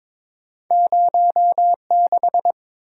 Morse code, heard as a keyed tone, about 30 wpm, spaced as for 26 wpm, 700 Hz. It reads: 06
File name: Speed-Racing - Two Letter-Number Combinations - 22wpm